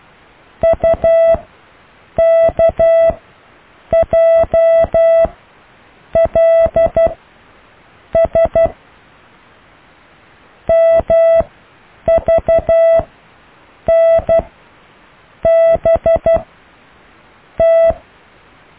Morse code, heard as a keyed tone, about 12 wpm, spaced as for 7 wpm, 660 Hz. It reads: UKJLS MVNBT
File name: cw / Alle letters